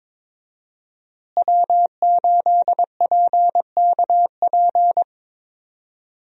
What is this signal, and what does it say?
Morse code, heard as a keyed tone, about 22 wpm, 700 Hz. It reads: W8PKP
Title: USA Call Signs - 5 Characters - Repeated 2x - 22wpm